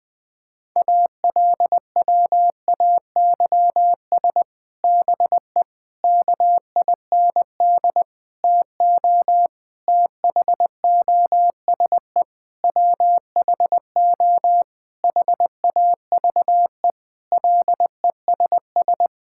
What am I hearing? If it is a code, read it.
Morse code, heard as a keyed tone, about 20 wpm, 700 Hz. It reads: ALWAYS BE KIND TO THOSE WHO HAVE LESS
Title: Sentences from Top 1600 Words 20wpm